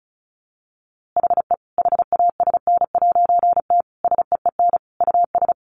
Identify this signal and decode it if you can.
Morse code, heard as a keyed tone, about 34 wpm, 700 Hz. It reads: HE HASN'T SEEN US
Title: A Sound Of Thunder 35wpm